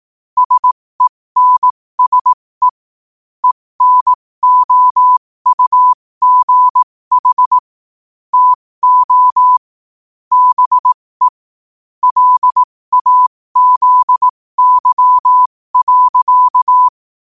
Morse code, heard as a keyed tone, about 18 wpm, spaced as for 15 wpm, 1000 Hz. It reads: SENSE ENOUGH TO BE LAZY.
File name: qotdfiles / 15wpm